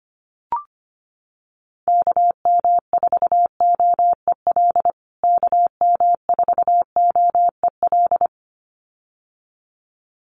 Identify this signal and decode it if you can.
Morse code, heard as a keyed tone, about 25 wpm, 700 Hz. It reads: KM4OEL KM4OEL
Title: USA Call Signs Repeated 2x 25wpm